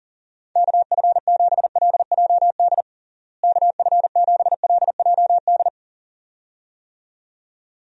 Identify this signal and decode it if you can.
Morse code, heard as a keyed tone, about 37 wpm, 700 Hz. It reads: KF7LJD KF7LJD
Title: USA Call Signs - 6 Characters - Repeated 2x - 40wpm